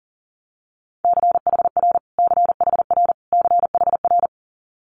Morse code, heard as a keyed tone, about 40 wpm, 700 Hz. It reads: CHR CHR CHR
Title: Multi-letter Phonemes Repeated 3x 40wpm